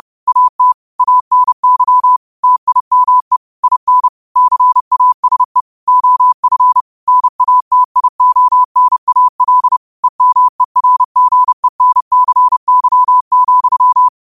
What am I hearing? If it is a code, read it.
Morse code, heard as a keyed tone, about 30 wpm, 1000 Hz. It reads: AT ANY TIME IN CASE OF NATIONAL EMERGENCY,
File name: qotdfiles / 30wpm